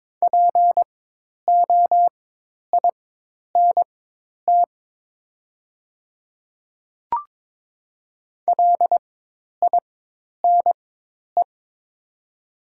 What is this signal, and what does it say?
Morse code, heard as a keyed tone, about 22 wpm, spaced as for 10 wpm, 700 Hz. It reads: POINT LINE
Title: 4x Character Spacing - Top 200 Words - 22wpm